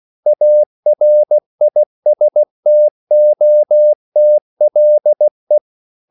Morse code, heard as a keyed tone, about 16 wpm, 600 Hz. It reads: ARISTOTLE
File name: CW / Practice / EnglishQuotations0216